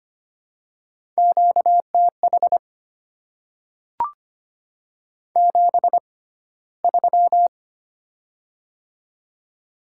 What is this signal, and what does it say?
Morse code, heard as a keyed tone, about 25 wpm, spaced as for 15 wpm, 700 Hz. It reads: QTH 73
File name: Top 20 QSO Elements - 6x Character Spacing - 25wpm